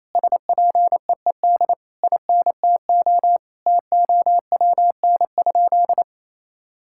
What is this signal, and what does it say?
Morse code, heard as a keyed tone, about 28 wpm, 700 Hz. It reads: SPEED INTO TOWN?
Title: Sentences from Top 1000 Words - Review Part 5 - 28wpm